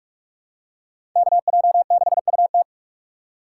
Morse code, heard as a keyed tone, about 44 wpm, 700 Hz. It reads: KJ6UT